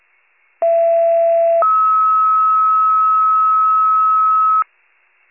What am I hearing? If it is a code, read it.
Two-tone page, heard as a one-second tone, then a three-second tone, about 670 then 1290 Hz. It is heard from 0.6 to 4.6 s.